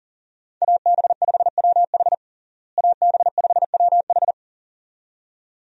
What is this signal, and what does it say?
Morse code, heard as a keyed tone, about 40 wpm, 700 Hz. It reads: AB5WH AB5WH
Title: USA Call Signs - 5 Characters - Repeated 2x - 40wpm